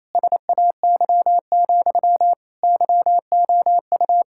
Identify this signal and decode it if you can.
Morse code, heard as a keyed tone, about 28 wpm, 700 Hz. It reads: SAY, YOU
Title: Sentences from Top 1600 Words 28wpm